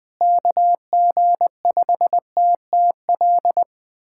Morse code, heard as a keyed tone, about 20 wpm, 700 Hz. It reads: KG5TTL
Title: USA Callsign DE Callsign 20wpm